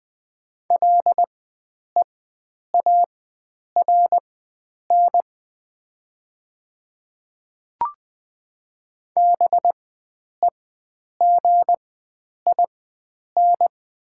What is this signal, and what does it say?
Morse code, heard as a keyed tone, about 20 wpm, spaced as for 9 wpm, 700 Hz. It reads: LEARN BEGIN
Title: 4x Character Spacing - Top 300 Words - 20wpm